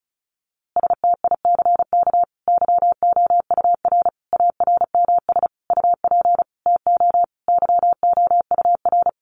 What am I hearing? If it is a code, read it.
Morse code, heard as a keyed tone, about 34 wpm, 700 Hz. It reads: STICK YOUR ARMS UP TO YOUR
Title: A Sound Of Thunder 35wpm